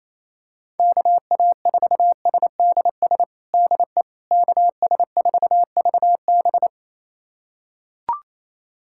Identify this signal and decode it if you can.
Morse code, heard as a keyed tone, about 28 wpm, 700 Hz. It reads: KA4SDS DE KS4VB
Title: USA Callsign DE Callsign 28wpm